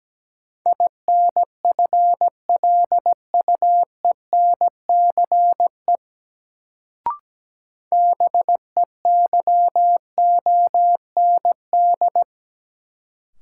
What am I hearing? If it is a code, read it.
Morse code, heard as a keyed tone, about 17 wpm, 700 Hz. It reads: INFLUENCE BEYOND
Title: Sentences from Top 1100 Words - New Word Drill Alt - 17wpm